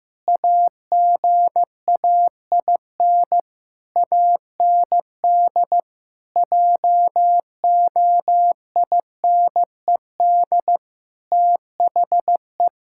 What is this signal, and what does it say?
Morse code, heard as a keyed tone, about 15 wpm, 700 Hz. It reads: AGAIN AND JOINED THE
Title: Pangram Challenge v2 - 15wpm